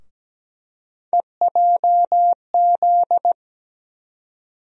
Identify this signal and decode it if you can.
Morse code, heard as a keyed tone, about 17 wpm, 700 Hz. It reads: EJZ
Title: USA Call Sign Postfixes - 3 Letters 17wpm